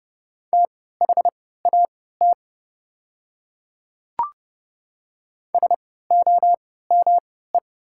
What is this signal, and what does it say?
Morse code, heard as a keyed tone, about 30 wpm, spaced as for 17 wpm, 700 Hz. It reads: THAT SOME
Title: 3x Character Spacing - Top 200 Words - 30wpm